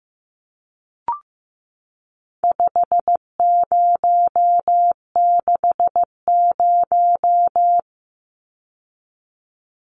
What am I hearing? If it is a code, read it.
Morse code, heard as a keyed tone, about 15 wpm, 700 Hz. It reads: 5060
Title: Numbers - 4-Digit 15wpm